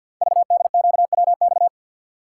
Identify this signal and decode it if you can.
Morse code, heard as a keyed tone, about 53 wpm, 700 Hz. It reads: VDQWX
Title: Code Groups - 5 Letter:Numbers - 50wpm